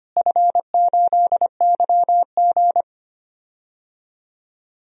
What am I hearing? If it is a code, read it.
Morse code, heard as a keyed tone, about 25 wpm, 700 Hz. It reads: F8YG
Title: DXCC-100 Call Signs - 4 Characters - Repeated 3x - 25wpm